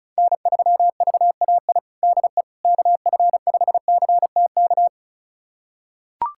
Morse code, heard as a keyed tone, about 34 wpm, 700 Hz. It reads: N3VAI DE KF5CTK
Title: USA Callsign DE Callsign 35wpm